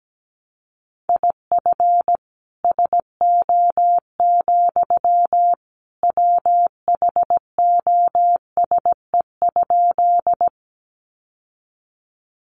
Morse code, heard as a keyed tone, about 17 wpm, 700 Hz. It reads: IF SO, WHOSE?
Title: Sentences from Top 1200 Words 17wpm